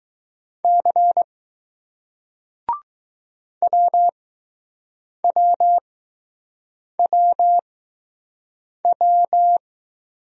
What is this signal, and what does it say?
Morse code, heard as a keyed tone, about 20 wpm, spaced as for 6 wpm, 700 Hz. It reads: C WWWW